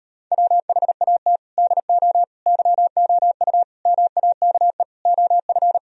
Morse code, heard as a keyed tone, about 38 wpm, 700 Hz. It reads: WHAT DO YOU MAKE OF